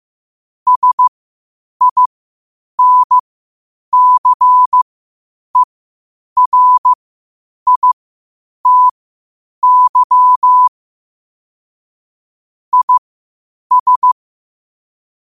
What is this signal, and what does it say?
Morse code, heard as a keyed tone, about 15 wpm, spaced as for 8 wpm, 1000 Hz. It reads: SINCERITY IS
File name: qotdfiles / 8wpm